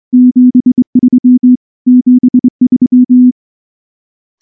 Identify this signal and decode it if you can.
Morse code, heard as a keyed tone, about 23 wpm, 250 Hz. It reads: 73 73
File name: morse / morse-training-25WPM